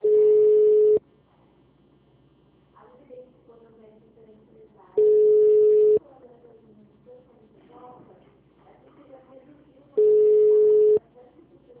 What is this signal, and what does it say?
Ringback tone (Europe, 425 Hz), with speech.